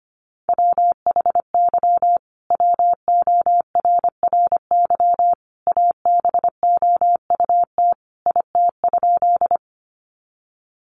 Morse code, heard as a keyed tone, about 25 wpm, 700 Hz. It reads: WHY WORRY ABOUT IT?
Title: Sentences from Top 1100 Words 25wpm